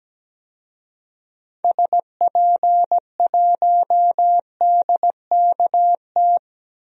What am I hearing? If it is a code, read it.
Morse code, heard as a keyed tone, about 17 wpm, 700 Hz. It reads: SP1DKT
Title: DXCC-100 Call Signs - 6 Characters - 17wpm